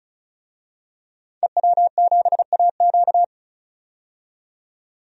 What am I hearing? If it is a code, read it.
Morse code, heard as a keyed tone, about 36 wpm, 700 Hz. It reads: EW7AQ
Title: DXCC-100 Call Signs - 5 Characters - 35wpm